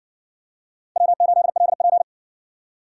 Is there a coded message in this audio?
Yes: Morse code from 1.0 to 2.0 s, heard as a keyed tone, about 60 wpm, 700 Hz. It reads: W8LP